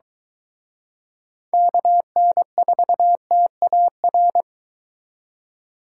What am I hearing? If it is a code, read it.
Morse code, heard as a keyed tone, about 23 wpm, 700 Hz. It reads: KN4TAR